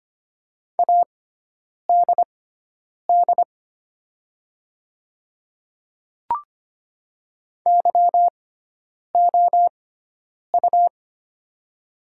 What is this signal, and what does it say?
Morse code, heard as a keyed tone, about 25 wpm, spaced as for 9 wpm, 690 Hz. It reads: ADD YOU
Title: 6x Character Spacing - Top 200 Words - 25wpm